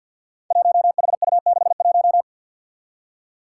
Morse code, heard as a keyed tone, about 52 wpm, 700 Hz. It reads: 1HU61
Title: Five Challenging Letter-Number Combinations 50wpm